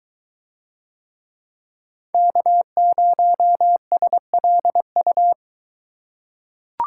Morse code, heard as a keyed tone, about 21 wpm, 700 Hz. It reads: K0SLU